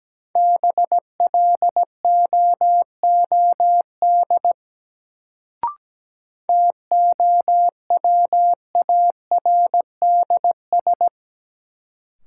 Morse code, heard as a keyed tone, about 17 wpm, 700 Hz. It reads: BLOOD TOWARDS
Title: Sentences from Top 1100 Words - New Word Drill Alt - 17wpm